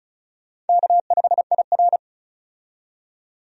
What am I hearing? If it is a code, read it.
Morse code, heard as a keyed tone, about 36 wpm, 700 Hz. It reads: K5IR
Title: USA Call Signs - 4 Characters - 35wpm